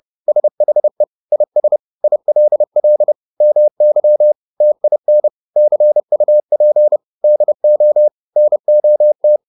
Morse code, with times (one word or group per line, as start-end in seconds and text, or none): 0.3-1.0 SHE
1.3-1.8 IS
2.0-3.1 ILL
3.4-4.3 MY
4.6-5.3 TIN
5.6-7.0 CUP
7.2-8.1 DO
8.4-9.4 NOT